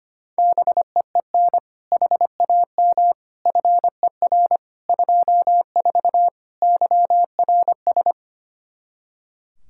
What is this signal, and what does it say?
Morse code, heard as a keyed tone, about 25 wpm, 700 Hz. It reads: BEEN HAM FER 24 YRS
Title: Practice QSO - Exchange 3 - More Info - 25wpm